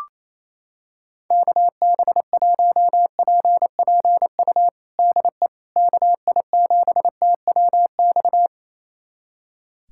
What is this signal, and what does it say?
Morse code, heard as a keyed tone, about 28 wpm, 700 Hz. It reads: KB1PPU DE KI7TWX
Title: USA Callsign DE Callsign - 6 Characters - 28wpm